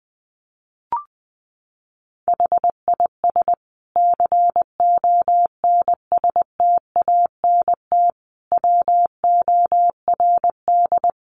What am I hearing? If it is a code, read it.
Morse code, heard as a keyed tone, about 20 wpm, 700 Hz. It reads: HIS CONSTANT WORD
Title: Sentences from Top 1500 Words 20wpm